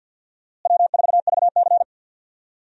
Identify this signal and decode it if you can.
Morse code, heard as a keyed tone, about 52 wpm, 700 Hz. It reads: W4VC
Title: USA Call Signs - 4 Characters - Repeated 3x - 50wpm